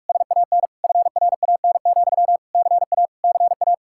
Morse code, heard as a keyed tone, about 49 wpm, 700 Hz. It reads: SAN FRAN, CA CA